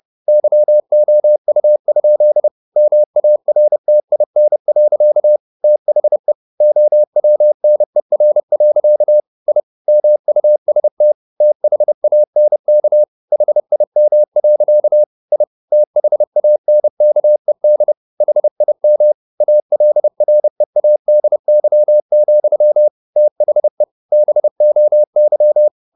Morse code, heard as a keyed tone, about 30 wpm, 600 Hz. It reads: YOU? MARTIN. THE OWNER. I MUST THANK HIM. I THANKED HIM ALREADY, THE BOY
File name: the old man and the sea / the old man and the sea 1 30wpm